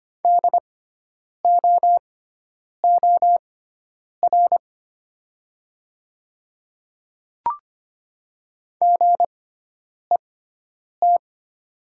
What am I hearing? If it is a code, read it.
Morse code, heard as a keyed tone, about 25 wpm, spaced as for 9 wpm, 700 Hz. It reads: DOOR GET
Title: 6x Character Spacing - Top 300 Words - 25wpm